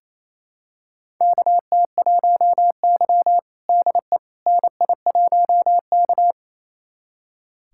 Morse code, heard as a keyed tone, about 28 wpm, 700 Hz. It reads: KT1Y DE NI1K
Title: USA Callsign DE Callsign - 4 Characters - 28wpm